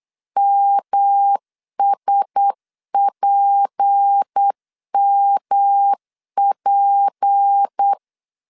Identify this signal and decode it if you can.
Morse code, heard as a keyed tone, about 8 wpm, 800 Hz. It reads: MSPMP
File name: LEZIONI / 6 PKX0 / 6 P 8wpm 3025